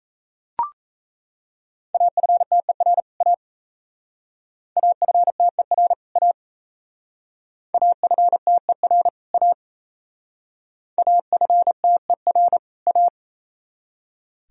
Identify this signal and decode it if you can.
Morse code, heard as a keyed tone, about 36 wpm, 700 Hz. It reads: AFTER A AFTER A AFTER A AFTER A